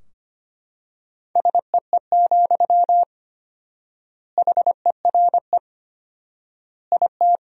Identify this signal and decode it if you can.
Morse code, heard as a keyed tone, about 25 wpm, 700 Hz. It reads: SEE, HERE IT
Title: 4x Word-Spacing - Sentences from Top 200 Words - 25wpm